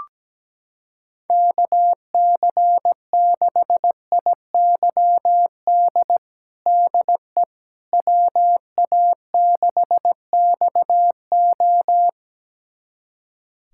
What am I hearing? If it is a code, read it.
Morse code, heard as a keyed tone, about 17 wpm, 700 Hz. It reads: KC6IYD DE WA6XO
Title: USA Callsign DE Callsign 17wpm